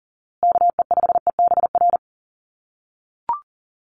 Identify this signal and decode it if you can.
Morse code, heard as a keyed tone, about 40 wpm, 700 Hz. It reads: KE5EBR
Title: USA Call Signs Repeated 2x 40wpm